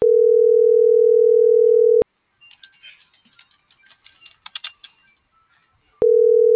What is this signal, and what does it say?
Ringback tone (North America), 440 + 480 Hz.